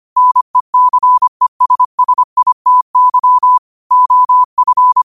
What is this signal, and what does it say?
Morse code, heard as a keyed tone, about 25 wpm, 1000 Hz. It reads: NECESSITY OF